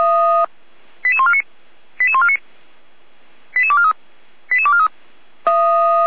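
Five-tone selective call (ZVEI), heard as quick runs of five tones, tones of about 70 ms. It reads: 88149 88149 88243 88243
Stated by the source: Received on SDR as ZVEI five-tone selective call 88245.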